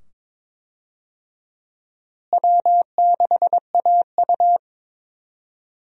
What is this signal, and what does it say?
Morse code, heard as a keyed tone, about 22 wpm, 700 Hz. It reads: W6AU
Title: USA Call Signs - 4 Characters - Repeated 2x - 22wpm